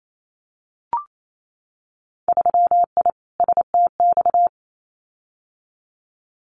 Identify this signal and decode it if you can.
Morse code, heard as a keyed tone, about 28 wpm, 700 Hz. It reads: 3I STX
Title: Winter Field Day Exchanges Encore 28wpm